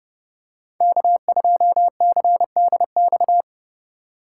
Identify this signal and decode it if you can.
Morse code, heard as a keyed tone, about 30 wpm, 700 Hz. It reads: K2CDX
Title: K1USN SST QSO Practice - 30wpm